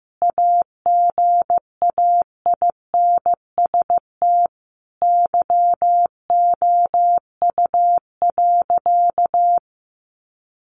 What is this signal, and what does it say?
Morse code, heard as a keyed tone, about 15 wpm, 700 Hz. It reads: AGAINST YOU.
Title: Sentences from Top 1200 Words 15wpm